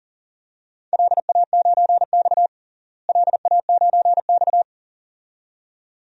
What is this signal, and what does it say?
Morse code, heard as a keyed tone, about 40 wpm, 700 Hz. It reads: LA9X LA9X
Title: DXCC-100 Call Signs - 4 Characters - Repeated 2x - 40wpm